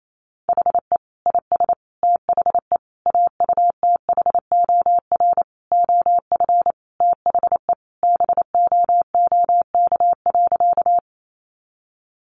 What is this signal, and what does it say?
Morse code, heard as a keyed tone, about 28 wpm, 700 Hz. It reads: HE IS THE AUTHOR OF THE BOOK.
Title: Sentences from Top 1200 Words 28wpm